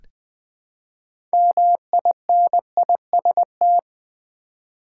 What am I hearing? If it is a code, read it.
Morse code, heard as a keyed tone, about 20 wpm, 700 Hz. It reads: MINIST
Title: Build A Word - 3 Letters At A Time - 8-Letter Words - 20wpm